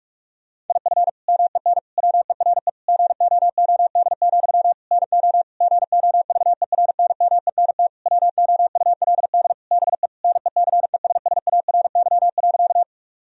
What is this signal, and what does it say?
Morse code, heard as a keyed tone, about 46 wpm, 700 Hz. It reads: IF MEN WERE GOOD, NO GOVERNMENT WOULD BE NECESSARY.
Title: Sentences from Top 1000 Words - Review Part 3 - 45wpm